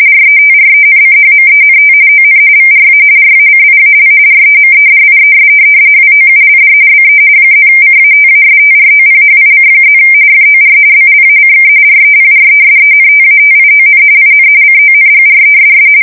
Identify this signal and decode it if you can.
RTTY, about 45.45 baud. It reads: RETRENCHMENT ON VRA RACIAL REMEDIES WOULD BE +CATASTROPHIC+ FOR BLACK REPRESENTATION IN CONGRESS
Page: 170 Shift 2125/2295